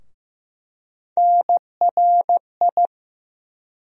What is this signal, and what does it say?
Morse code, heard as a keyed tone, about 15 wpm, 700 Hz. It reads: NRI
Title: USA Call Sign Postfixes - 3 Letters 15wpm